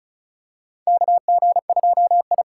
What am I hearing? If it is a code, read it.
Morse code, heard as a keyed tone, about 36 wpm, 700 Hz. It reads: KG2I
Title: USA Call Signs - 4 Characters - 2x Character Spacing - 35wpm